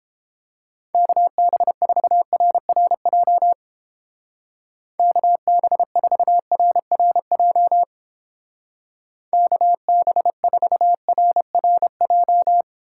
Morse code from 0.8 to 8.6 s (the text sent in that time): KB4RRJ KB4RRJ